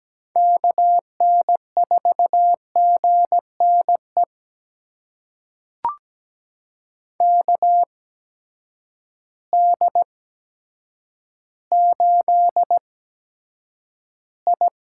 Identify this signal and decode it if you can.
Morse code, heard as a keyed tone, about 17 wpm, spaced as for 8 wpm, 700 Hz. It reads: KN4GNE KD8I
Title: USA Call Signs - 8x Character Spacing - 17wpm